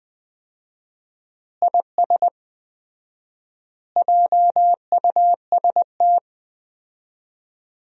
Morse code, heard as a keyed tone, about 20 wpm, 700 Hz. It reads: IS JUST
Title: 4x Word-Spacing - Sentences from Top 200 Words - 20wpm